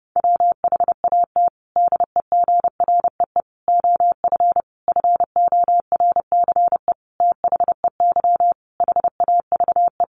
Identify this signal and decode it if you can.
Morse code, heard as a keyed tone, about 28 wpm, 700 Hz. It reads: WHAT DEGREE OF FORCE THEY HAVE
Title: Sentences from Top 1500 Words 30wpm